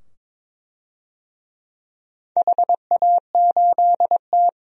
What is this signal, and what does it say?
Morse code, heard as a keyed tone, about 21 wpm, 700 Hz. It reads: HA8T
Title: DXCC-100 Call Signs - 4 Characters - 22wpm